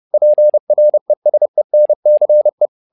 Morse code, heard as a keyed tone, about 30 wpm, 600 Hz. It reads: PRESENCE
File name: learning center / tower trouble / tower trouble 30wpm